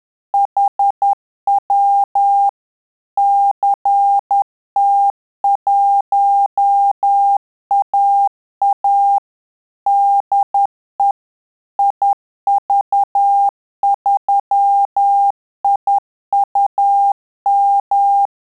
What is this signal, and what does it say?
Morse code, heard as a keyed tone, about 10 wpm, 800 Hz. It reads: HW CT1AA DE IV3IUM
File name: Telegraphy-group / QSO CW 10 PPM